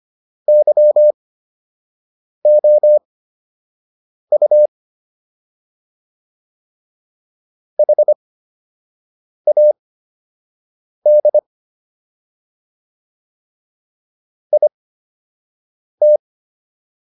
Morse code, heard as a keyed tone, about 25 wpm, spaced as for 6 wpm, 600 Hz. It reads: YOU HAD IT